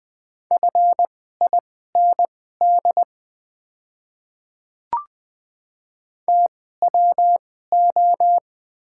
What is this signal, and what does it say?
Morse code, heard as a keyed tone, about 20 wpm, spaced as for 14 wpm, 700 Hz. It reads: FIND TWO
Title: 2x Character Spacing - Top 300 Words - 20wpm